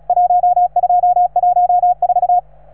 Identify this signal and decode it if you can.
Morse code, heard as a keyed tone, about 38 wpm, 720 Hz. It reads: 1214